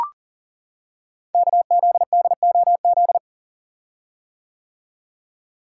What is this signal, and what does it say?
Morse code, heard as a keyed tone, about 39 wpm, 700 Hz. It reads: KZDOZ
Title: Code Groups - 5 Letter:Numbers - 40wpm